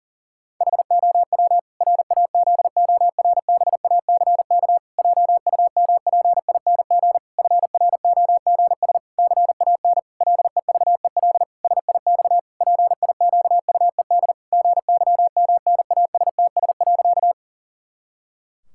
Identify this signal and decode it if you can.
Morse code, heard as a keyed tone, about 40 wpm, 700 Hz. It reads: HOW RAZORBACK JUMPING FROGS CAN LEVEL SIX PIQUED GYMNASTS.
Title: Pangram Challenge v2 - 40wpm